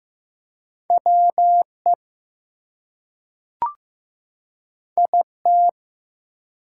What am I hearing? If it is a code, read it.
Morse code, heard as a keyed tone, about 15 wpm, 700 Hz. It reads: WE IT
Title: Two Letter Words - 15wpm